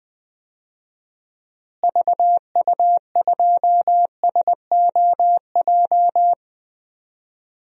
Morse code, heard as a keyed tone, about 20 wpm, 700 Hz. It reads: VU2SOJ
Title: DXCC-100 Call Signs - 6 Characters - Repeated 2x - 20wpm